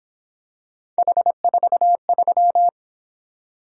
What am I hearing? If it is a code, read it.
Morse code, heard as a keyed tone, about 26 wpm, 700 Hz. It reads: H43
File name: Speed-Racing - Three Letter-Number Combinations - 22wpm